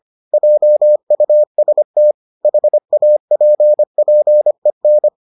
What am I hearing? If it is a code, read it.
Morse code, heard as a keyed tone, about 25 wpm, 600 Hz. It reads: JUST HAPPEN